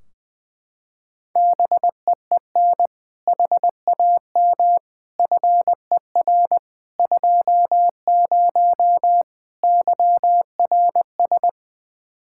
Morse code, heard as a keyed tone, about 21 wpm, 700 Hz. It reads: BEEN HAM FER 20 YRS
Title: Practice QSO - Exchange 3 - More Info - 20wpm